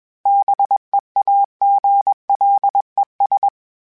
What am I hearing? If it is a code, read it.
Morse code, heard as a keyed tone, about 21 wpm, 800 Hz. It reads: BEAGLES